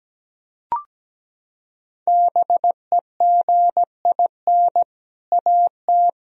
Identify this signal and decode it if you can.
Morse code, heard as a keyed tone, about 17 wpm, 700 Hz. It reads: BEGIN AT
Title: Sentences from Top 1000 Words - Review Part 1 - 17wpm